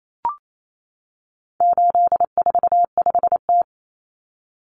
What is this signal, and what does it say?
Morse code, heard as a keyed tone, about 28 wpm, 700 Hz. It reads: 845T
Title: Common Cut Numbers - 4-Digit 28wpm